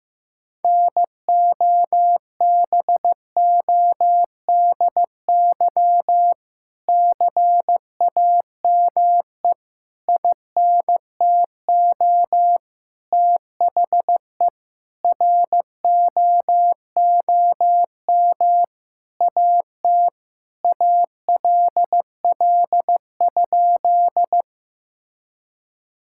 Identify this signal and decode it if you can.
Morse code, heard as a keyed tone, about 15 wpm, 700 Hz. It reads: NOBODY CAME INTO THE ROOM AT ALL?
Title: Sentences from Top 1600 Words 15wpm